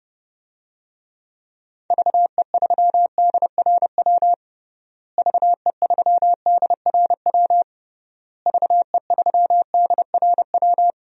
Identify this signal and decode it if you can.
Morse code, heard as a keyed tone, about 32 wpm, 700 Hz. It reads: VE3DRW VE3DRW VE3DRW
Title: DXCC-100 Call Signs - 6 Characters - Repeated 3x - 30wpm